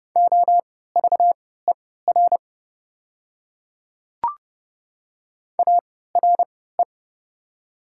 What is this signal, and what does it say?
Morse code, heard as a keyed tone, about 29 wpm, spaced as for 17 wpm, 690 Hz. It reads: OVER ARE
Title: 3x Character Spacing - Top 100 Words - 30wpm